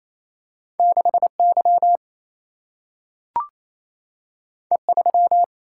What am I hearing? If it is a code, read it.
Morse code, heard as a keyed tone, about 28 wpm, 700 Hz. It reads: 6Y E3
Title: Single-Letter Single-Number 28wpm